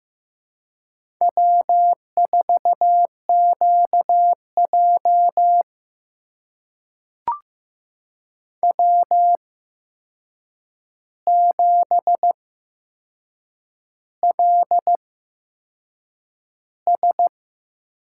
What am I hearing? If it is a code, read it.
Morse code, heard as a keyed tone, about 15 wpm, spaced as for 6 wpm, 700 Hz. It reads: W4QJ W7LS
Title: USA Call Signs - 4 Characters - 8x Character Spacing - 15wpm